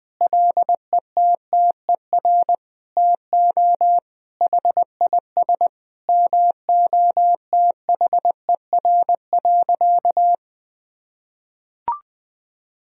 Morse code, heard as a keyed tone, about 20 wpm, 700 Hz. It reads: LETTER TO HIS MOTHER.
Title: Sentences from Top 1000 Words - Review Part 3 - 20wpm